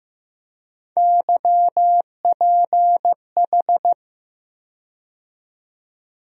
Morse code, heard as a keyed tone, about 15 wpm, 700 Hz. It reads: YPH
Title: USA Call Sign Postfixes - 3 Letters 15wpm